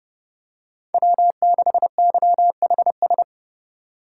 Morse code, heard as a keyed tone, about 30 wpm, 700 Hz. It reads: W6YHS
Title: USA Call Signs - 5 Characters - 3x Character Spacing - 30wpm